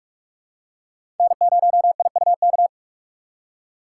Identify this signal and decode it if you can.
Morse code, heard as a keyed tone, about 46 wpm, 700 Hz. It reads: N0IUK
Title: USA Call Signs - 5 Characters - 3x Character Spacing - 45wpm